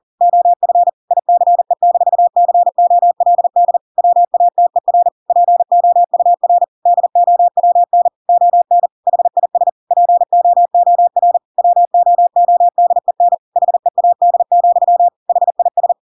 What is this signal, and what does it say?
Morse code, heard as a keyed tone, about 40 wpm, 700 Hz. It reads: OF ICE-COLD WATER POUR DOWN ON HIS POOR WOODEN HEAD, HIS
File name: Pinocchio / Pinocchio ch6 40 40 ik1hge